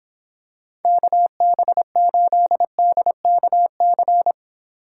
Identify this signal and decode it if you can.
Morse code, heard as a keyed tone, about 25 wpm, 700 Hz. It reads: KB8DKC